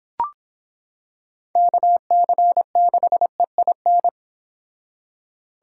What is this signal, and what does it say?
Morse code, heard as a keyed tone, about 25 wpm, 700 Hz. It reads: KC6EIN